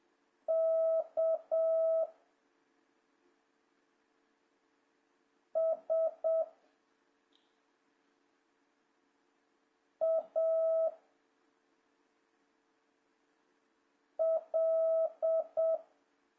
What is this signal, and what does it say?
Morse code, heard as a keyed tone, about 7 wpm, 650 Hz. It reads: K S A L
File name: jh8cbh / 2026cw